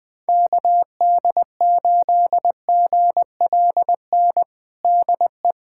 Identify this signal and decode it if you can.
Morse code, heard as a keyed tone, about 20 wpm, 700 Hz. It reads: KD8GLN DE
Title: USA Callsign DE Callsign 20wpm